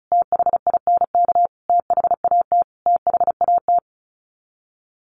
Morse code, heard as a keyed tone, about 34 wpm, 700 Hz. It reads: THINK THAT THAT
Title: Sets of 5 Words Encore 35wpm